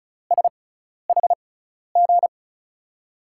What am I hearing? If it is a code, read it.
Morse code, heard as a keyed tone, about 36 wpm, spaced as for 12 wpm, 700 Hz. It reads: SHG
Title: USA Call Signs - 6x Character Spacing - 35wpm